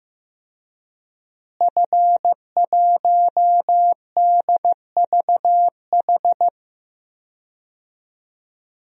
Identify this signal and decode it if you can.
Morse code, heard as a keyed tone, about 15 wpm, 700 Hz. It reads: F1DVH
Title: DXCC-100 Call Signs - 5 Characters - Repeated 2x - 15wpm